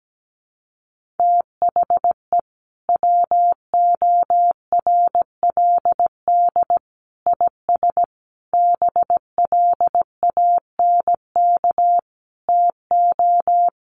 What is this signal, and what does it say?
Morse code, heard as a keyed tone, about 17 wpm, 700 Hz. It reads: THE WORLD IS BLANK TO
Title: Sentences from Top 1500 Words 17wpm